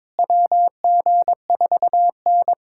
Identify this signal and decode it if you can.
Morse code, heard as a keyed tone, about 21 wpm, 700 Hz. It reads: WG4N
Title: USA Callsign DE Callsign - 4 Characters - 22wpm